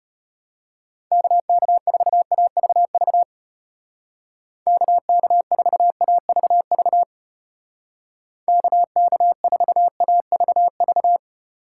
Morse code, heard as a keyed tone, about 34 wpm, 700 Hz. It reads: KK4AVV KK4AVV KK4AVV